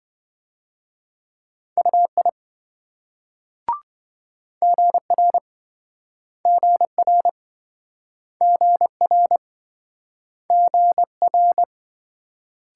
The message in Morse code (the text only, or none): UI GR GR GR GR